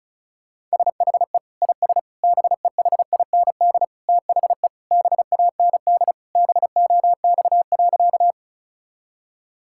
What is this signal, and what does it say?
Morse code, heard as a keyed tone, about 36 wpm, 700 Hz. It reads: SHE IS BEHIND THE BAND BOX.
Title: Sentences from Top 1600 Words 35wpm